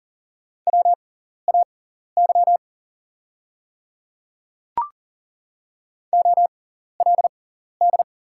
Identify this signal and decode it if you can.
Morse code, heard as a keyed tone, about 40 wpm, spaced as for 14 wpm, 700 Hz. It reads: WAY OLD
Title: 6x Character Spacing - Top 200 Words - 40wpm